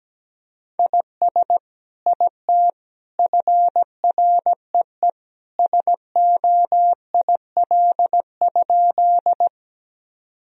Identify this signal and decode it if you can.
Morse code, heard as a keyed tone, about 17 wpm, 700 Hz. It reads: IS IT FREE SOIL?
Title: Sentences from Top 1000 Words - Review Part 4 - 17wpm